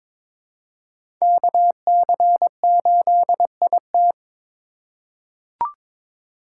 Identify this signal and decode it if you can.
Morse code, heard as a keyed tone, about 22 wpm, 700 Hz. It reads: KC8IT
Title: USA Call Signs - 4x Character Spacing - 22wpm